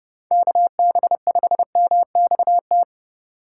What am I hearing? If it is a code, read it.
Morse code, heard as a keyed tone, about 29 wpm, 700 Hz. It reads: KB5MXT